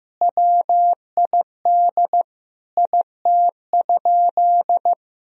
Morse code, heard as a keyed tone, about 15 wpm, 700 Hz. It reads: WID IT?
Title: Sentences from Top 100 Words and Abbreviations 15wpm